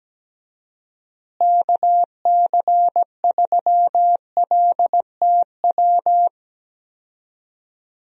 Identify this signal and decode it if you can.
Morse code, heard as a keyed tone, about 17 wpm, 700 Hz. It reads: KC3LTW
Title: USA Call Signs - 6 Characters - Repeated 3x - 17wpm